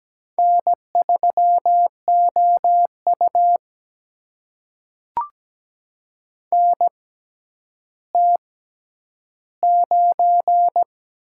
Morse code, heard as a keyed tone, about 17 wpm, spaced as for 10 wpm, 700 Hz. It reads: N3OU NT9
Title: USA Call Signs - 4 Characters - 6x Character Spacing - 17wpm